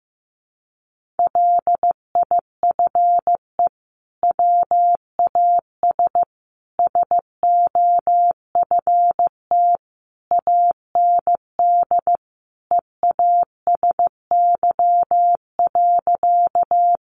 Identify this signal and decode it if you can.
Morse code, heard as a keyed tone, about 15 wpm, 700 Hz. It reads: LIFE WAS SOFT AND EASY.
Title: Sentences from Top 1500 Words 15wpm